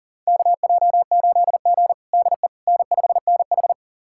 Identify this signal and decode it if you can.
Morse code, heard as a keyed tone, about 40 wpm, 700 Hz. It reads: KJ8G DE N5NH
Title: USA Callsign DE Callsign - 4 Characters - 40wpm